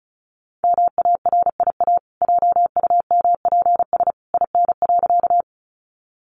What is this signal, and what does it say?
Morse code, heard as a keyed tone, about 34 wpm, 700 Hz. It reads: MARIA JUMPS IN.
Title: Easy Phrases 35wpm